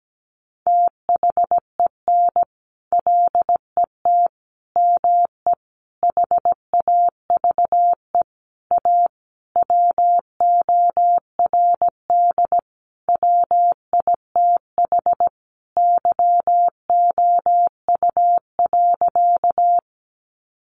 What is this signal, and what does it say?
Morse code, heard as a keyed tone, about 17 wpm, 700 Hz. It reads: THEN LET ME HAVE A WORD WITH YOU.
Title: Sentences from Top 300 Words 17wpm